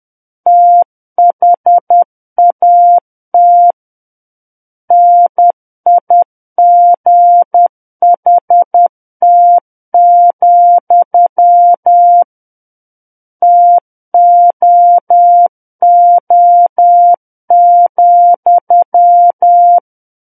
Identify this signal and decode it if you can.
Morse code, heard as a keyed tone, about 10 wpm, 700 Hz. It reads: THAT NIGHT, TOO,
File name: Book One / 01-Chapter 1 - 10 WPM